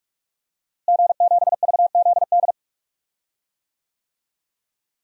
Morse code, heard as a keyed tone, about 44 wpm, 700 Hz. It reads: G7VZD
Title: DXCC-100 Call Signs - 5 Characters - Repeated 3x - 45wpm